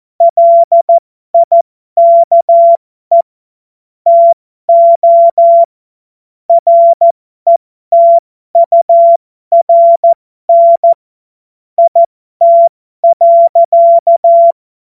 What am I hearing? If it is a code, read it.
Morse code, heard as a keyed tone, about 14 wpm, spaced as for 12 wpm, 680 Hz. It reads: LIKE TO RETURN IT.
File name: Morsekurs / Witze